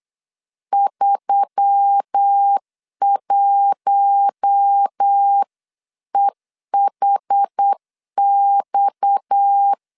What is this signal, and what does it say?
Morse code, heard as a keyed tone, about 8 wpm, 800 Hz. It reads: 31EHX
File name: CW / DET / 1 9 1500C S A 8wpm 5637